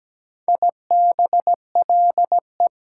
Morse code, heard as a keyed tone, about 17 wpm, 700 Hz. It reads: IBLE
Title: Prefixes and Suffixes 17wpm